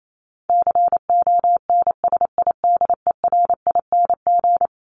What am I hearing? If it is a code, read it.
Morse code, heard as a keyed tone, about 28 wpm, 700 Hz. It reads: CONSIDERING
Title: ING Suffix - Top 100 Words with Base Words 28wpm